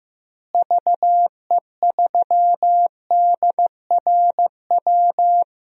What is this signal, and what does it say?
Morse code, heard as a keyed tone, about 15 wpm, 700 Hz. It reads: VE3DRW
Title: DXCC-100 Call Signs - 6 Characters - Repeated 3x - 15wpm